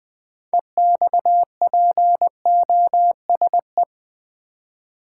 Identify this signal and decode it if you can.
Morse code, heard as a keyed tone, about 20 wpm, 700 Hz. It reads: EXPOSE
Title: Six Letter Words - 20wpm